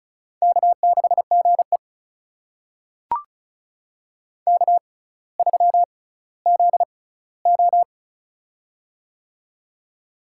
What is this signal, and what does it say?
Morse code, heard as a keyed tone, about 36 wpm, spaced as for 18 wpm, 700 Hz. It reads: K6GE K3ZO
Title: USA Call Signs - 4 Characters - 6x Character Spacing - 35wpm